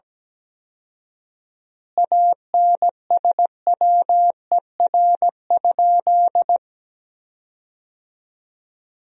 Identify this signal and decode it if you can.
Morse code, heard as a keyed tone, about 17 wpm, 700 Hz. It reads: ANSWER?
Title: 4x Word-Spacing - Sentences from Top 300 Words - 17wpm